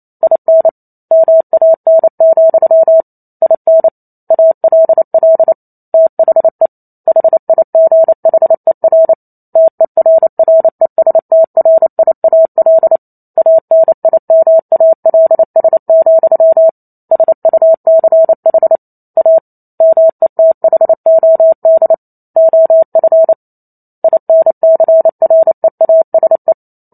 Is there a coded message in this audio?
Yes: Morse code from 0.2 to 26.5 s, heard as a keyed tone, about 28 wpm, 650 Hz. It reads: IN MAN, IN ALL THE HIGHER TERRESTRIAL ANIMALS, SUCH A METHOD OF INCREASE